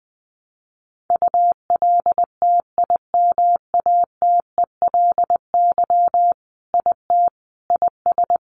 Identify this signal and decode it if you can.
Morse code, heard as a keyed tone, about 20 wpm, 700 Hz. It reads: ULTIMATELY IT IS
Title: Sentences from Top 1200 Words 20wpm